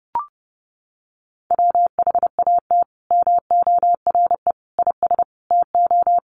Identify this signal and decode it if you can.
Morse code, heard as a keyed tone, about 30 wpm, 700 Hz. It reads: WHAT MORE IS TO
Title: Sentences from Top 500 Words 30wpm